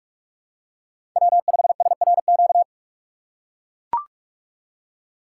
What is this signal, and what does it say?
Morse code, heard as a keyed tone, about 44 wpm, 700 Hz. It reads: W5SRQ
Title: USA Call Signs - 5 Characters - 3x Character Spacing - 45wpm